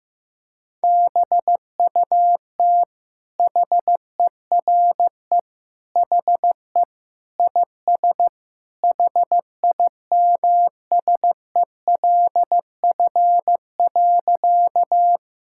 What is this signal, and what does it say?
Morse code, heard as a keyed tone, about 16 wpm, 700 Hz. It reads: BUT HERE HE IS HIMSELF.
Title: Sentences from Top 1600 Words 15wpm